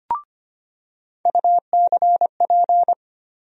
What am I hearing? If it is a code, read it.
Morse code, heard as a keyed tone, about 25 wpm, 700 Hz. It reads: UCP
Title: USA Call Sign Postfixes - 3 Letters 25wpm